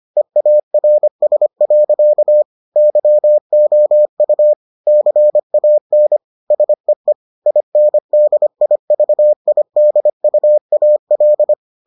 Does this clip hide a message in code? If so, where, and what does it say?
Yes: Morse code from 0.2 to 11.5 s, heard as a keyed tone, about 25 wpm, 600 Hz. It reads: EARS. YOU CAN SEE INDIVIDUAL